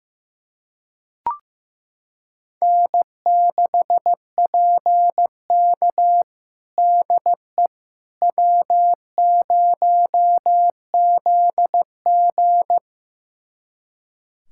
Morse code, heard as a keyed tone, about 16 wpm, 700 Hz. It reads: N6PK DE W0ZG
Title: USA Callsign DE Callsign - 4 Characters - 15wpm